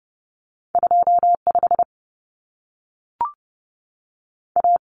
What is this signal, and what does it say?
Morse code, heard as a keyed tone, about 30 wpm, 700 Hz. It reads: 25 A
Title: Common Cut Numbers - 2-Digit 30wpm